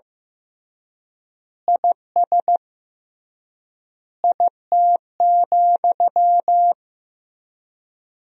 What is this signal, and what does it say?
Morse code, heard as a keyed tone, about 15 wpm, 700 Hz. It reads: IS IT,
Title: 3x Word-Spacing - Sentences from Top 300 Words - 15wpm